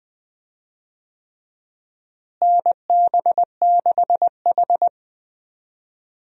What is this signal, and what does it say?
Morse code, heard as a keyed tone, about 20 wpm, 700 Hz. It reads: NB6H
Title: USA Call Signs - 4 Characters - Repeated 3x - 20wpm